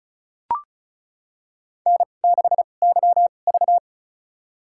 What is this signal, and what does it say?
Morse code, heard as a keyed tone, about 36 wpm, spaced as for 25 wpm, 700 Hz. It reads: N6YV
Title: USA Call Signs - 4 Characters - 2x Character Spacing - 35wpm